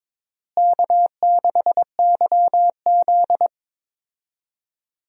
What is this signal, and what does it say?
Morse code, heard as a keyed tone, about 22 wpm, 700 Hz. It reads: K6YZ
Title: USA Call Signs - 4 Characters - Repeated 3x - 22wpm